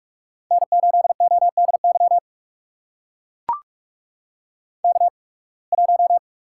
Morse code, heard as a keyed tone, about 44 wpm, spaced as for 29 wpm, 690 Hz. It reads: N8ODY K1
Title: USA Call Signs - 5 Characters - 8x Character Spacing - 45wpm